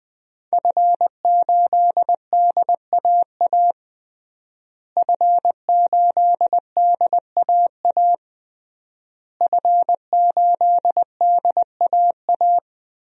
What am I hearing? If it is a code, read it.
Morse code, heard as a keyed tone, about 19 wpm, 700 Hz. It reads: F8DAA F8DAA F8DAA
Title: DXCC-100 Call Signs - 5 Characters - Repeated 3x - 20wpm